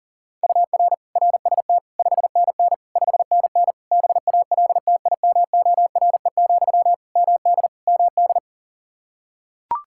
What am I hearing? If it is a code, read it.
Morse code, heard as a keyed tone, about 40 wpm, 700 Hz. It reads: UR RST 5NN 5NN BALTIMORE, MD MD
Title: Practice QSO - Exchange 1 - Beginning - 40wpm